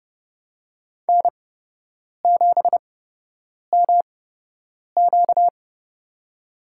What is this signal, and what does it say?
Morse code, heard as a keyed tone, about 30 wpm, spaced as for 8 wpm, 700 Hz. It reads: N7MQ
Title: USA Call Signs - 4 Characters - 8x Character Spacing - 30wpm